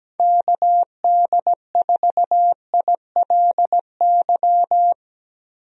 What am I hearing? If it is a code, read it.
Morse code, heard as a keyed tone, about 17 wpm, 700 Hz. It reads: KD4ILY
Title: USA Call Signs - 6 Characters - 17wpm